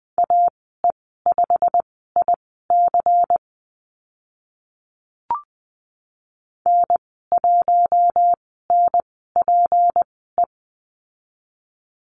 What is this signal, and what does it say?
Morse code, heard as a keyed tone, about 20 wpm, spaced as for 15 wpm, 700 Hz. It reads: AE5IC N1NPE
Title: 193 Lesson 31g - Letter v - Call Signs 20wpm